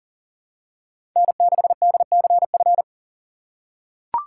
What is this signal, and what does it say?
Morse code, heard as a keyed tone, about 40 wpm, 700 Hz. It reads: N6DCF
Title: USA Call Signs - 5 Characters - 4x Character Spacing - 40wpm